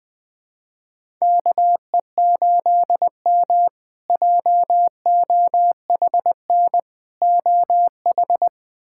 Morse code, heard as a keyed tone, about 20 wpm, 700 Hz. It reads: KE8M JOHN OH
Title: K1USN SST QSO Practice - 20wpm